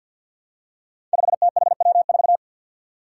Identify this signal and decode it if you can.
Morse code, heard as a keyed tone, about 48 wpm, 700 Hz. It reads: 5THW4
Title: Five Challenging Letter-Number Combinations 50wpm